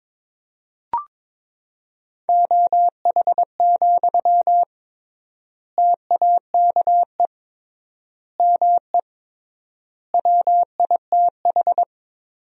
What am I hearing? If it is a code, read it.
Morse code, heard as a keyed tone, about 22 wpm, 700 Hz. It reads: OH, TAKE ME WITH
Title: 3x Word-Spacing - Sentences from Top 500 Words - 22wpm